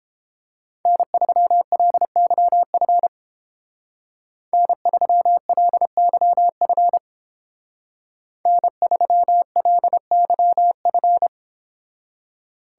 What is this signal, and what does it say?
Morse code, heard as a keyed tone, about 29 wpm, 700 Hz. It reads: N3LYF N3LYF N3LYF